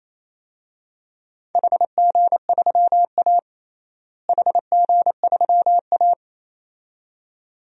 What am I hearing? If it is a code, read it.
Morse code, heard as a keyed tone, about 28 wpm, 700 Hz. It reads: HG3A HG3A
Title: DXCC-100 Call Signs - 4 Characters - Repeated 2x - 28wpm